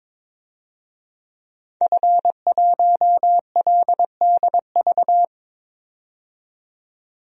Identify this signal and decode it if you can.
Morse code, heard as a keyed tone, about 22 wpm, 700 Hz. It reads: F1LDV
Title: DXCC-100 Call Signs - 5 Characters - 22wpm